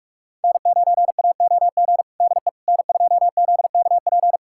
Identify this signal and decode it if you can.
Morse code, heard as a keyed tone, about 46 wpm, 700 Hz. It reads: N9AOG DE N2ZKP
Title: USA Callsign DE Callsign - 5 Characters - 45wpm